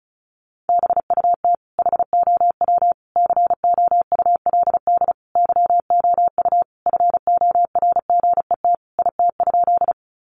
Morse code, heard as a keyed tone, about 34 wpm, 700 Hz. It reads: BUT HOW COULD YOU FORGET IT?
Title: Sentences from Top 1200 Words 35wpm